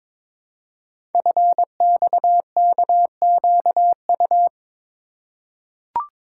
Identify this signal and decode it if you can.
Morse code, heard as a keyed tone, about 22 wpm, 700 Hz. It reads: FXKQU
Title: Code Groups - 5 Letters - 22wpm